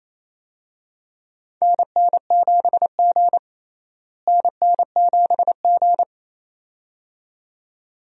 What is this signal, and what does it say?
Morse code, heard as a keyed tone, about 28 wpm, 700 Hz. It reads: NN7G NN7G
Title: USA Call Signs - 4 Characters - Repeated 2x - 28wpm